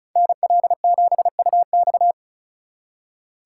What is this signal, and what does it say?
Morse code, heard as a keyed tone, about 45 wpm, 700 Hz. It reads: NL7UX